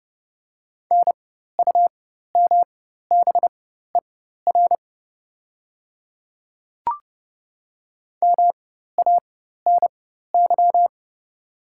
Morse code, heard as a keyed tone, about 30 wpm, spaced as for 14 wpm, 690 Hz. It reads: NUMBER MANY
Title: 4x Character Spacing - Top 100 Words - 30wpm